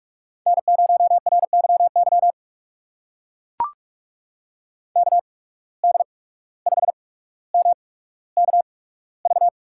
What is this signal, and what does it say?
Morse code, heard as a keyed tone, about 44 wpm, spaced as for 18 wpm, 700 Hz. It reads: N0RYY KD5MKV
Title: USA Call Signs - 8x Character Spacing - 45wpm